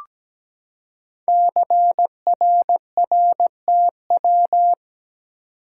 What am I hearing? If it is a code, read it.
Morse code, heard as a keyed tone, about 17 wpm, 700 Hz. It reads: CRRTW
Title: Code Groups - 5 Letters - 17wpm